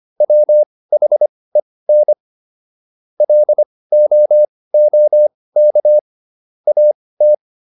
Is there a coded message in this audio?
Yes: Morse code from 0.2 to 7.3 s, heard as a keyed tone, about 25 wpm, spaced as for 18 wpm, 600 Hz. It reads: WHEN LOOK AT